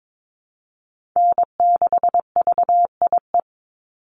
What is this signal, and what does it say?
Morse code, heard as a keyed tone, about 22 wpm, 700 Hz. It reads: N6VIE
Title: USA Call Signs 22wpm